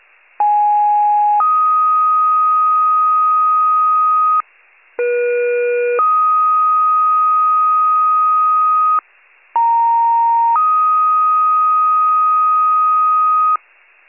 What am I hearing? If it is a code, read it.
Two-tone page, heard as a one-second tone, then a three-second tone, about 830 then 1250 Hz, about 480 then 1250 Hz, about 930 then 1250 Hz. It is heard from 0.4 to 13.6 s.
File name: wcalert